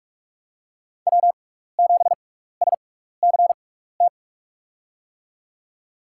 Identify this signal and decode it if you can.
Morse code, heard as a keyed tone, about 46 wpm, spaced as for 15 wpm, 700 Hz. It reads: W7SCT
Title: USA Call Signs - 5 Characters - 6x Character Spacing - 45wpm